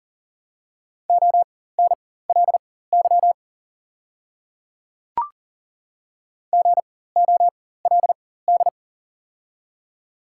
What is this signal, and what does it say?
Morse code, heard as a keyed tone, about 39 wpm, spaced as for 19 wpm, 690 Hz. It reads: ONLY GOLD
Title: 4x Character Spacing - Top 500 Words - 40wpm